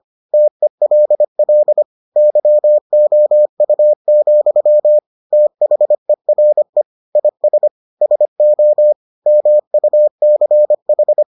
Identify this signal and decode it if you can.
Morse code, heard as a keyed tone, about 25 wpm, 600 Hz. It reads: TELL YOU, THERE IS SO MUCH